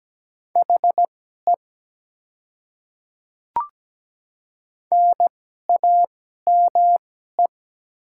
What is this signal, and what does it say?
Morse code, heard as a keyed tone, about 17 wpm, spaced as for 12 wpm, 700 Hz. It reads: HE NAME
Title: 2x Character Spacing - Top 200 Words - 17wpm